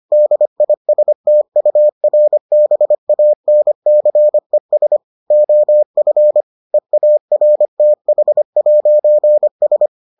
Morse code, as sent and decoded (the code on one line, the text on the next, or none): -.. .. ... - ..- .-. -... .- -. -.-. ... / --- ..-. / . .- .-. - .... .----. ...
DISTURBANCES OF EARTH'S